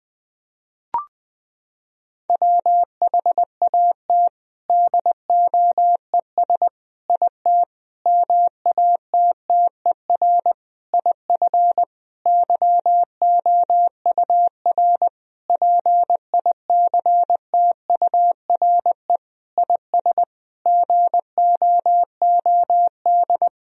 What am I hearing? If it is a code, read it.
Morse code, heard as a keyed tone, about 20 wpm, 700 Hz. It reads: WHAT DOES IT MATTER IF YOUR PICTURE IS GOOD
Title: Sentences from Top 1000 Words - Review Part 1 - 20wpm